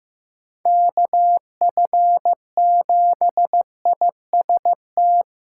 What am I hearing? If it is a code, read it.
Morse code, heard as a keyed tone, about 15 wpm, 700 Hz. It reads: KF7IST
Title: USA Callsign DE Callsign - 6 Characters - 15wpm